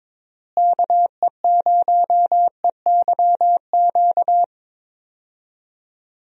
Morse code, heard as a keyed tone, about 21 wpm, 700 Hz. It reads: KE0EYQ
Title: USA Call Signs - 6 Characters - Repeated 2x - 22wpm